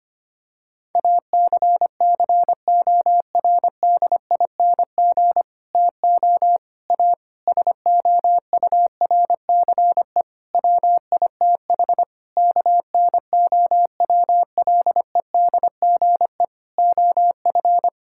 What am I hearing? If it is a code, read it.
Morse code, heard as a keyed tone, about 25 wpm, 700 Hz. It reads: ACCORDING TO A SOURCE WITH KNOWLEDGE OF
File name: practice / 25wpm / Sentences with 9-Word Phrases - 25wpm